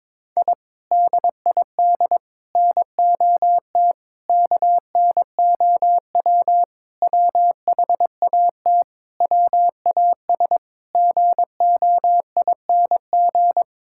Morse code, heard as a keyed tone, about 22 wpm, 700 Hz. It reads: I DID NOT KNOW WHAT WAS GOING
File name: practice / 22wpm / Sentences with 4-Word Phrases - 22wpm